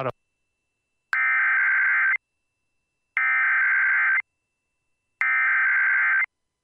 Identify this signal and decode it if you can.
Emergency Alert System (SAME) tones (header bursts), heard as harsh buzzing bursts, with speech. It is heard from 1.1 to 6.2 s.